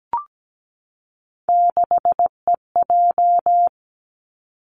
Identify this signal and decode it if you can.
Morse code, heard as a keyed tone, about 17 wpm, 700 Hz. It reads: 6EJ
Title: Three Challenging Letter-Number Combinations 17wpm